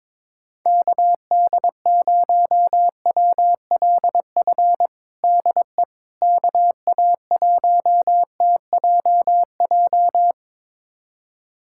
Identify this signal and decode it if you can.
Morse code, heard as a keyed tone, about 22 wpm, 700 Hz. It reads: KD0WLF DE KA1TJJ
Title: USA Callsign DE Callsign - 6 Characters - 22wpm